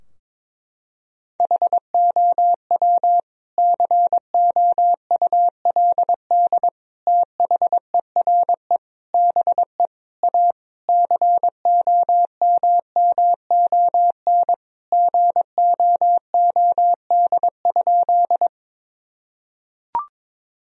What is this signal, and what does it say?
Morse code, heard as a keyed tone, about 22 wpm, 700 Hz. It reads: HOW COULD THERE BE A COMMON GOOD?
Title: Sentences from Top 1000 Words - Review Part 3 - 22wpm